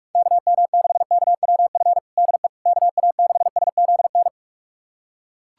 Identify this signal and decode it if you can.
Morse code, heard as a keyed tone, about 47 wpm, 700 Hz. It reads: KM6KWF DE KA6SZN
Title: USA Callsign DE Callsign - 6 Characters - 45wpm